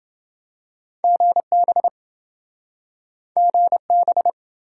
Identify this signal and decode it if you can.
Morse code, heard as a keyed tone, about 26 wpm, 700 Hz. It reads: GB GB